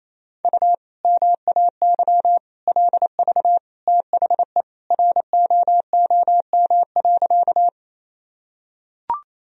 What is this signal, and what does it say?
Morse code, heard as a keyed tone, about 28 wpm, 700 Hz. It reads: U MAY LV THE ROOM.
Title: Sentences from Top 400 Words and Abbreviations 28wpm